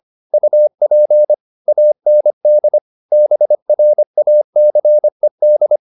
Morse code, heard as a keyed tone, about 25 wpm, 600 Hz. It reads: UP AND BRACED
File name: the old man and the sea part 1 25wpm